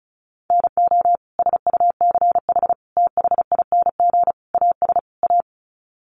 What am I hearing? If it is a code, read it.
Morse code, heard as a keyed tone, about 34 wpm, 700 Hz. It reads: NO SUCH THING AS A
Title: Sets of 5 Words Encore 35wpm